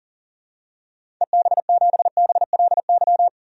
Morse code, heard as a keyed tone, about 40 wpm, 700 Hz. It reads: EB7BLY
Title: DXCC-100 Call Signs - 6 Characters - 40wpm